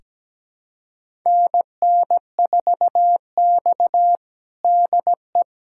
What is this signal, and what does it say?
Morse code, heard as a keyed tone, about 17 wpm, 700 Hz. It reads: NN4X DE
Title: USA Callsign DE Callsign - 4 Characters - 17wpm